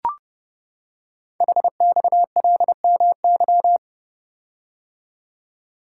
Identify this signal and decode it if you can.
Morse code, heard as a keyed tone, about 30 wpm, 700 Hz. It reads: HXLMY
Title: Code Groups - 5 Letters - 30wpm